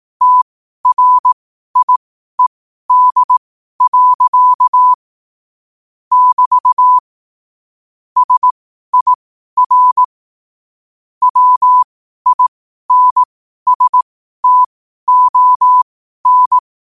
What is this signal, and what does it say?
Morse code, heard as a keyed tone, about 18 wpm, spaced as for 12 wpm, 1000 Hz. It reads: TRIED. <BT> SIR WINSTON
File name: qotdfiles / 12wpm